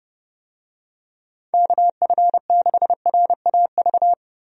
Morse code, heard as a keyed tone, about 30 wpm, 700 Hz. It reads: KF6RAV
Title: USA Call Signs - 6 Characters - 30wpm